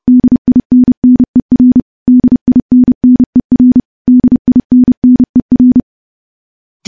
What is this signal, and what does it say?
Morse code, heard as a keyed tone, about 30 wpm, 250 Hz. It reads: DINNER DINNER DINNER
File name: morse-training-30WPM